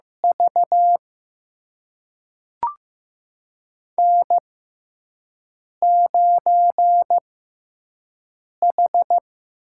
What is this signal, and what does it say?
Morse code, heard as a keyed tone, about 15 wpm, spaced as for 5 wpm, 700 Hz. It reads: V N9H
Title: USA Call Signs - 5 Characters - 6x Character Spacing - 15wpm